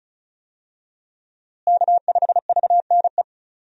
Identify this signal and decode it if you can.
Morse code, heard as a keyed tone, about 36 wpm, 700 Hz. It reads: K5VNE
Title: USA Call Signs - 5 Characters - Repeated 2x - 35wpm